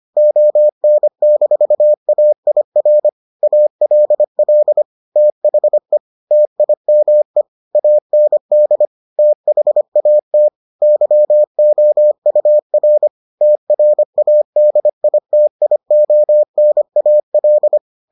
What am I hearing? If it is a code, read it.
Morse code, heard as a keyed tone, about 25 wpm, 600 Hz. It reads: ON-AIR ALL THE TIME AND THAT YOUR TRADITIONAL